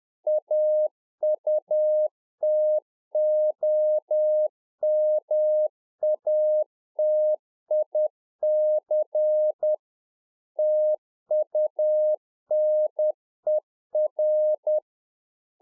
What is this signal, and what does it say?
Morse code, heard as a keyed tone, about 10 wpm, 610 Hz. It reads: AUTOMATIC TUNER
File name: UNFOGETTABLE QSO 10WPM